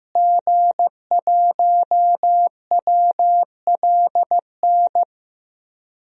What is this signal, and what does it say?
Morse code, heard as a keyed tone, about 15 wpm, 700 Hz. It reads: G1WLN
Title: DXCC-100 Call Signs - 5 Characters - Repeated 3x - 15wpm